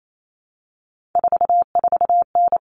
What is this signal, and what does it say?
Morse code, heard as a keyed tone, about 28 wpm, 700 Hz. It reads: 44N
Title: Common Cut Numbers - 3-Digit 28wpm